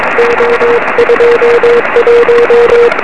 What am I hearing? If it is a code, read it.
Morse code, heard as a keyed tone, about 22 wpm, 470 Hz, with radio noise. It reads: O21
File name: 2111z pa2m 23cm